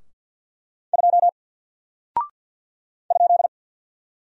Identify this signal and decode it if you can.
Morse code, heard as a keyed tone, about 48 wpm, 700 Hz. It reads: 2 ?
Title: Single Character (Letter:Number:Punctuation:Prosign) Alt 50wpm